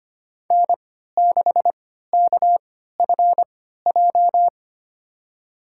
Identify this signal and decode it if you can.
Morse code, heard as a keyed tone, about 25 wpm, spaced as for 14 wpm, 700 Hz. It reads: N6KFJ
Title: USA Call Signs - 3x Character Spacing - 25wpm